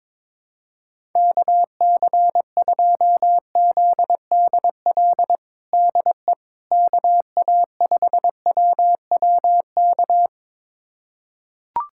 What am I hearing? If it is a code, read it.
Morse code, heard as a keyed tone, about 22 wpm, 700 Hz. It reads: KC2ZDL DE KA5WWK
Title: USA Callsign DE Callsign 22wpm